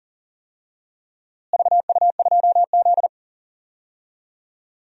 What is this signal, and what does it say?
Morse code, heard as a keyed tone, about 40 wpm, 700 Hz. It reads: VU2Z
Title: DXCC-100 Call Signs - 4 Characters - Repeated 3x - 40wpm